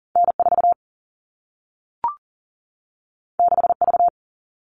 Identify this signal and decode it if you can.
Morse code, heard as a keyed tone, about 40 wpm, 700 Hz. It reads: N4 6V
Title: All Cut Numbers - 2-Digit 40wpm